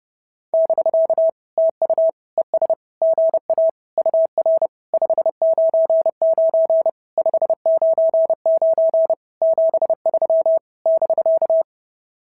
Morse code, heard as a keyed tone, about 30 wpm, 690 Hz. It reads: <BK> TU ES GA UR 599 599 73 <BK>
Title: SOTA QSO Practice 30wpm